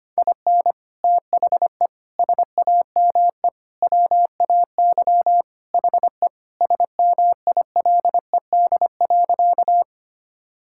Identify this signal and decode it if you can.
Morse code, heard as a keyed tone, about 25 wpm, 700 Hz. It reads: IN THE SAME WAY HE SMILED.
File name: practice / 25wpm / Sentences with 4-Word Phrases - 25wpm